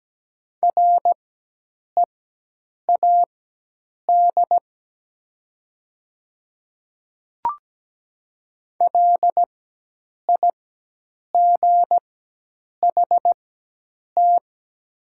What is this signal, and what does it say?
Morse code, heard as a keyed tone, about 17 wpm, spaced as for 8 wpm, 700 Hz. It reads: READ LIGHT
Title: 4x Character Spacing - Top 200 Words - 17wpm